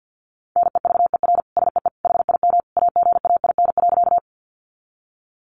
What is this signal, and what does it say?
Morse code, heard as a keyed tone, about 50 wpm, 700 Hz. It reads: NEVER SEE HIM AGAIN.
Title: Sentences from Top 300 Words 50wpm